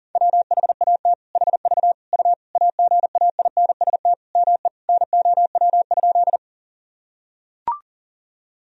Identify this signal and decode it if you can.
Morse code, heard as a keyed tone, about 40 wpm, 700 Hz. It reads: WHAT HV U AGAINST ME NOW?
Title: Sentences from Top 400 Words and Abbreviations 40wpm